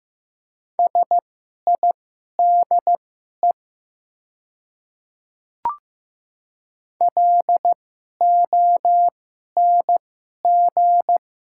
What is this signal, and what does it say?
Morse code, heard as a keyed tone, about 15 wpm, spaced as for 11 wpm, 700 Hz. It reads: SIDE LONG
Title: 2x Character Spacing - Top 100 Words - 15wpm